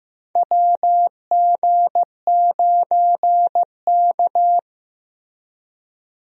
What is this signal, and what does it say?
Morse code, heard as a keyed tone, about 15 wpm, 700 Hz. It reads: WG9K
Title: USA Call Signs - 4 Characters - 15wpm